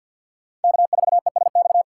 Morse code, heard as a keyed tone, about 53 wpm, 700 Hz. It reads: K4ESX
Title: USA Call Signs - 5 Characters - 2x Character Spacing - 50wpm